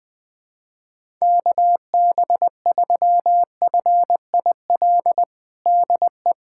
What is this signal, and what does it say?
Morse code, heard as a keyed tone, about 20 wpm, 700 Hz. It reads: KB3FIL DE
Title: USA Callsign DE Callsign - 6 Characters - 20wpm